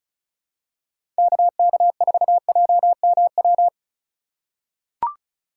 Morse code, heard as a keyed tone, about 36 wpm, 700 Hz. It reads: KK4JMW
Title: USA Call Signs - 6 Characters - 2x Character Spacing - 35wpm